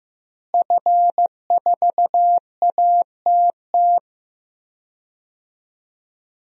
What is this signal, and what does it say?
Morse code, heard as a keyed tone, about 16 wpm, 700 Hz. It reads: F4ATT
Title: DXCC-100 Call Signs - 5 Characters - Repeated 3x - 15wpm